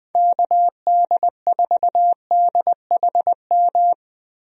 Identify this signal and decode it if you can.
Morse code, heard as a keyed tone, about 20 wpm, 700 Hz. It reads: KD4DHM
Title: USA Call Signs - 6 Characters - Repeated 2x - 20wpm